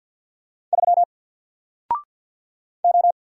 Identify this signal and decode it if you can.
Morse code, heard as a keyed tone, about 48 wpm, 700 Hz. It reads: 3 O
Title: Single Character (Letter:Number:Punctuation:Prosign) Alt 50wpm